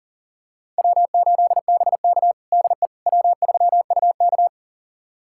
Morse code, heard as a keyed tone, about 40 wpm, 700 Hz. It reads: W8BK DE W3UK
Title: USA Callsign DE Callsign - 4 Characters - 40wpm